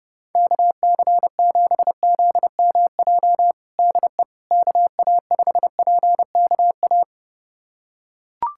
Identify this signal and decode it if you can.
Morse code, heard as a keyed tone, about 30 wpm, 700 Hz. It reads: KC7ZMJ DE KA5PKA
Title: USA Callsign DE Callsign - 6 Characters - 30wpm